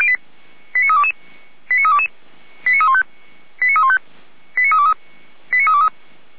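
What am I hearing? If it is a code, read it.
Five-tone selective call (ZVEI), heard as quick runs of five tones, tones of about 80 ms. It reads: 89322 89322 89315 89315 89323 89323
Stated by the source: ZVEI five-tone call received by SDR, code 89320.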